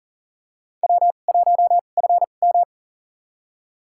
Morse code, heard as a keyed tone, about 40 wpm, spaced as for 29 wpm, 700 Hz. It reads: W1FM
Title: USA Call Signs - 4 Characters - 2x Character Spacing - 40wpm